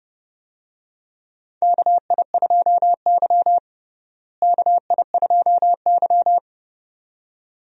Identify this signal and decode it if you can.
Morse code, heard as a keyed tone, about 30 wpm, 700 Hz. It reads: KI2Y KI2Y
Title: USA Call Signs - 4 Characters - Repeated 2x - 30wpm